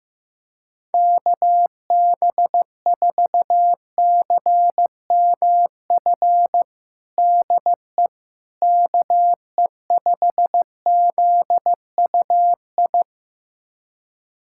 Morse code, heard as a keyed tone, about 15 wpm, 700 Hz. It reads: KB4CMF DE KE5ZUI
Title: USA Callsign DE Callsign 15wpm